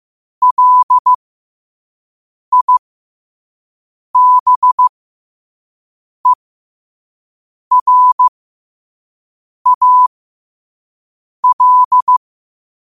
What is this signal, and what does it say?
Morse code, heard as a keyed tone, about 15 wpm, spaced as for 5 wpm, 1000 Hz. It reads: LIBERAL
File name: qotdfiles / 5wpm